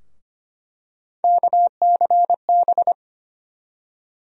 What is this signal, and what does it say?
Morse code, heard as a keyed tone, about 25 wpm, 700 Hz. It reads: KCB
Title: USA Call Sign Postfixes - 3 Letters 25wpm